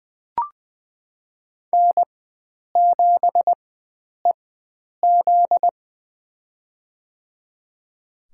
Morse code, heard as a keyed tone, about 20 wpm, spaced as for 9 wpm, 700 Hz. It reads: N7EZ
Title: USA Call Signs - 4 Characters - 4x Character Spacing - 20wpm